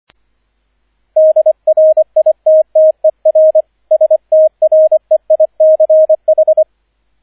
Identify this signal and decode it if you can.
Morse code, heard as a keyed tone, about 25 wpm, 630 Hz, with radio noise. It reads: DRITTER STREICH